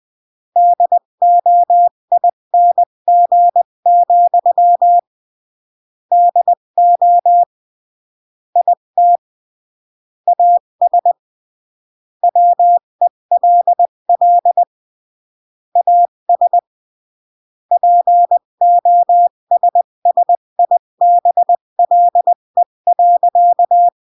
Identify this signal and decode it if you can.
Morse code, heard as a keyed tone, about 20 wpm, 700 Hz. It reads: DOING, DO IT AS WELL AS POSSIBLE.
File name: CW / jodorowsky / 82 commandments 20-15